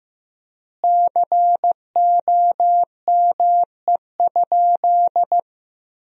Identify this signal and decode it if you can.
Morse code, heard as a keyed tone, about 16 wpm, 700 Hz. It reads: COME?
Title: 3x Word-Spacing - Sentences from Top 200 Words - 15wpm